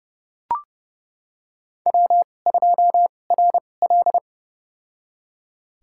Morse code, heard as a keyed tone, about 29 wpm, spaced as for 21 wpm, 700 Hz. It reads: W2RL
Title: USA Call Signs - 2x Character Spacing - 30wpm